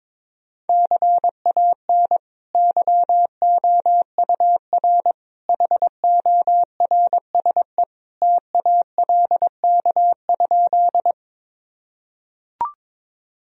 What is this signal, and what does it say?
Morse code, heard as a keyed tone, about 22 wpm, 700 Hz. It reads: CAN YOUR HORSE TALK?
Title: Sentences from Top 1000 Words - Review Part 1 - 22wpm